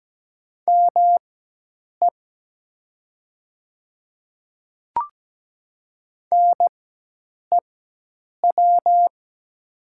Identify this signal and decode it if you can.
Morse code, heard as a keyed tone, about 17 wpm, spaced as for 8 wpm, 700 Hz. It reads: ME NEW
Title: 4x Character Spacing - Top 200 Words - 17wpm